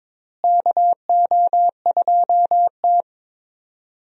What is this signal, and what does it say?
Morse code, heard as a keyed tone, about 22 wpm, 700 Hz. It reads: KO2T
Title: USA Call Signs - 4 Characters - Repeated 3x - 22wpm